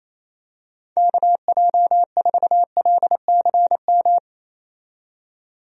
Morse code, heard as a keyed tone, about 28 wpm, 700 Hz. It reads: KJ4LCM
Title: USA Call Signs - 6 Characters - 28wpm